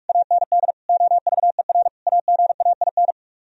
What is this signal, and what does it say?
Morse code, heard as a keyed tone, about 58 wpm, 700 Hz. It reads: AND OVER AGAIN